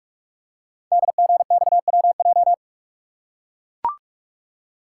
Morse code, heard as a keyed tone, about 44 wpm, 700 Hz. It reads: DGXWJ
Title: Code Groups - 5 Letters - 45wpm